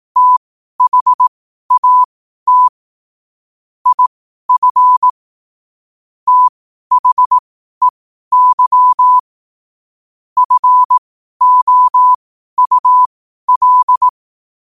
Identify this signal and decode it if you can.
Morse code, heard as a keyed tone, about 18 wpm, spaced as for 12 wpm, 1000 Hz. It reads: THAT IF THEY FOUL
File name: qotdfiles / 12wpm